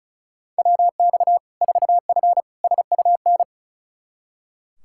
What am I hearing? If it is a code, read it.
Morse code, heard as a keyed tone, about 36 wpm, 700 Hz. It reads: WX 4F SUN
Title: Practice QSO - Exchange 2 - Info - 35wpm